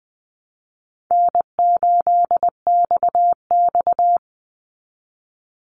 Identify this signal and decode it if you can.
Morse code, heard as a keyed tone, about 19 wpm, 700 Hz. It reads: N8XX
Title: Call Signs - 20wpm